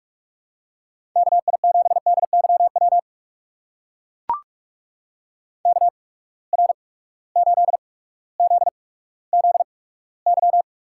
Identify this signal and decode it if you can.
Morse code, heard as a keyed tone, about 44 wpm, spaced as for 19 wpm, 700 Hz. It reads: KI7DYW KR8ZZY
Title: USA Call Signs - 8x Character Spacing - 45wpm